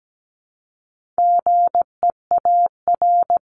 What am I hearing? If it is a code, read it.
Morse code, heard as a keyed tone, about 17 wpm, 700 Hz. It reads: GEAR
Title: Top 1251-1500 Words 17wpm